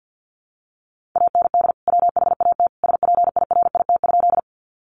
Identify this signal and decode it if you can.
Morse code, heard as a keyed tone, about 50 wpm, 700 Hz. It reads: AND WHAT SPIRIT?
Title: Sentences from Top 1400 Words 50wpm